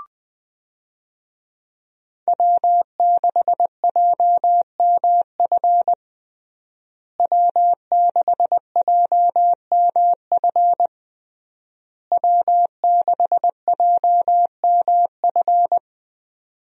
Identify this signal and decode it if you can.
Morse code, heard as a keyed tone, about 21 wpm, 700 Hz. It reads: W6JMF W6JMF W6JMF
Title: USA Call Signs - 5 Characters - Repeated 3x - 20wpm